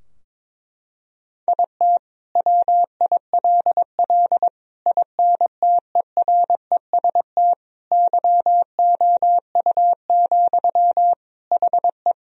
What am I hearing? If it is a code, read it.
Morse code, heard as a keyed tone, about 22 wpm, 700 Hz. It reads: IT WILL INTEREST YOU, HE
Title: Sentences from Top 1000 Words - Review Part 2 - 22wpm